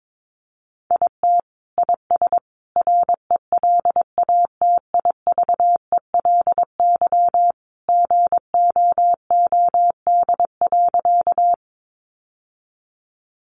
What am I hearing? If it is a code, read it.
Morse code, heard as a keyed tone, about 22 wpm, 700 Hz. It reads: IT IS RELATIVELY GOOD.
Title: Sentences from Top 1100 Words 22wpm